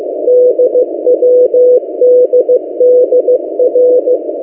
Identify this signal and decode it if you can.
Morse code, heard as a keyed tone, about 15 wpm, 500 Hz. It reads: DWDDR
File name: M08a- -1355utc-8096khz